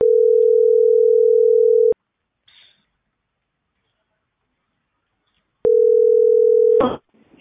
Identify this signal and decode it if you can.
Ringback tone (North America), 440 + 480 Hz.